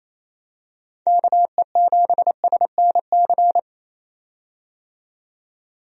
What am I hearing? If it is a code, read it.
Morse code, heard as a keyed tone, about 28 wpm, 700 Hz. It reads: KE7SNC
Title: USA Call Signs - 6 Characters - 28wpm